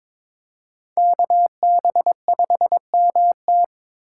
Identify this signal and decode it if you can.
Morse code, heard as a keyed tone, about 26 wpm, 700 Hz. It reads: KB5MT
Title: Speed Racing - USA 2x2 Call Signs 22wpm